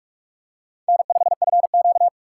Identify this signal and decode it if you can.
Morse code, heard as a keyed tone, about 41 wpm, 700 Hz. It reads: N5FQ
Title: USA Call Signs - 4 Characters - 6x Character Spacing - 45wpm